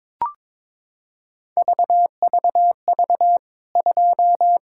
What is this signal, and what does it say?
Morse code, heard as a keyed tone, about 22 wpm, 700 Hz. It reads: VVV 2
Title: K1USN SST QSO Practice 22wpm